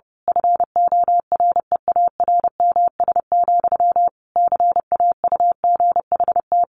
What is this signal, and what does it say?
Morse code, heard as a keyed tone, about 30 wpm, 700 Hz. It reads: FOREARMS, CAUGHT
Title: A Sound Of Thunder 30wpm